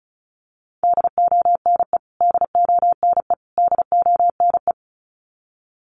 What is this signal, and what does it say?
Morse code, heard as a keyed tone, about 34 wpm, 700 Hz. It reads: DONE DONE DONE
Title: Top 500 Words Repeated 35wpm